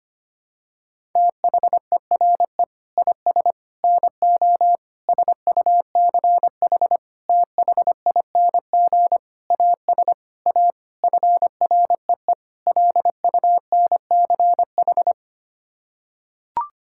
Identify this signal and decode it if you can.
Morse code, heard as a keyed tone, about 25 wpm, 700 Hz. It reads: THERE IS NO SUCH THING AS A FREE LUNCH
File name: Sets of 9 Words - 25wpm